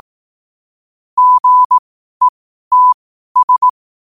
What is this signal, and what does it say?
Morse code, heard as a keyed tone, about 18 wpm, spaced as for 12 wpm, 1000 Hz. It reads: GETS
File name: qotdfiles / 12wpm